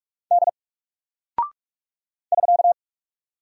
Morse code, heard as a keyed tone, about 46 wpm, 700 Hz. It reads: D <SK>
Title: Single Character (Letter:Number:Punctuation:Prosign) 45wpm